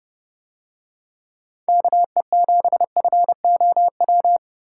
Morse code, heard as a keyed tone, about 30 wpm, 700 Hz. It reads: KE7FOW